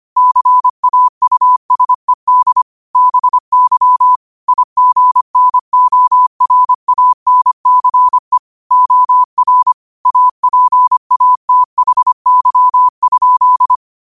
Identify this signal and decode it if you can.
Morse code, heard as a keyed tone, about 25 wpm, 1000 Hz. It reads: CAUSED BY IGNORANCE OR APATHY?
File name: 25wpm